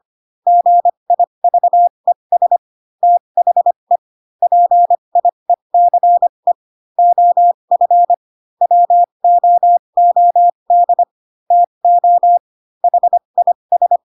Morse code, heard as a keyed tone, about 25 wpm, spaced as for 22 wpm, 700 Hz. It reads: GIVES THE PIECE OF WOOD TO HIS
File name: Pinocchio / Pinocchio ch2 18 25 ik1hge